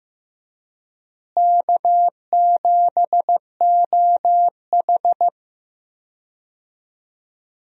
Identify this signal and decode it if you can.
Morse code, heard as a keyed tone, about 15 wpm, 700 Hz. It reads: K7OH
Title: USA Call Signs - 4 Characters - Repeated 3x - 15wpm